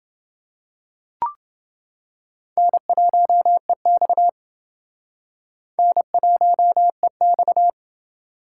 Morse code, heard as a keyed tone, about 31 wpm, 700 Hz. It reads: N1EX N1EX